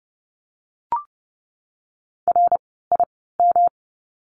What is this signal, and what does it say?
Morse code, heard as a keyed tone, about 30 wpm, spaced as for 17 wpm, 700 Hz. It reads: RIM
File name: course / 30:10wpm / 108 Lesson 18f - Letter m - Words 30wpm